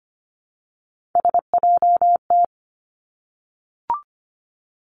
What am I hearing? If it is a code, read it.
Morse code, heard as a keyed tone, about 25 wpm, 700 Hz. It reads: SJT
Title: Three Challenging Letter-Number Combinations 25wpm